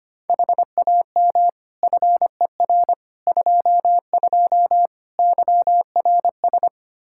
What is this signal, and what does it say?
Morse code, heard as a keyed tone, about 25 wpm, 700 Hz. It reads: HAM FER 22 YRS
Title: Practice QSO - Exchange 3 - More Info - 25wpm